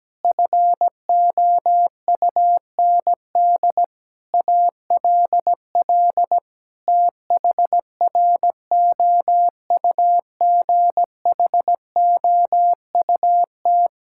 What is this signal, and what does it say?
Morse code, heard as a keyed tone, about 17 wpm, 700 Hz. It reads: FOUND ALL THROUGHOUT
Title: Sentences from Top 1000 Words - Review Part 5 - 17wpm